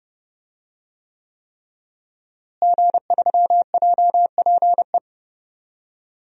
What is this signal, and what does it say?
Morse code, heard as a keyed tone, about 30 wpm, 700 Hz. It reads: G3JPE
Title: DXCC-100 Call Signs - 5 Characters - Repeated 3x - 30wpm